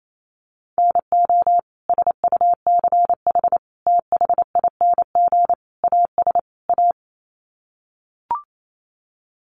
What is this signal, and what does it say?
Morse code, heard as a keyed tone, about 28 wpm, 700 Hz. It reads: NO SUCH THING AS A
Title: Sets of 5 Words Encore 28wpm